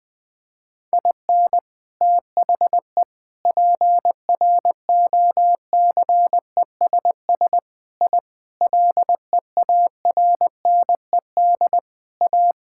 Morse code, heard as a keyed tone, about 20 wpm, 700 Hz. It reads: IN THE PROCESS I LEARNED A
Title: Sentences with 3-Word Phrases - Spoken Beginning - 20wpm